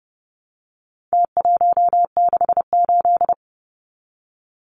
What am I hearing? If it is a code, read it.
Morse code, heard as a keyed tone, about 30 wpm, 700 Hz. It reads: T168
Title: Numbers - Contest Style - 4-Digit 30wpm